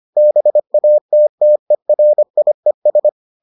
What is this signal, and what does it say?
Morse code, heard as a keyed tone, about 25 wpm, 600 Hz. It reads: BATTERIES